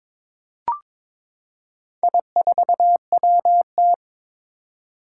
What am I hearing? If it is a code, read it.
Morse code, heard as a keyed tone, about 22 wpm, 700 Hz. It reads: I4WT
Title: Four Challenging Letter-Number Combinations 22wpm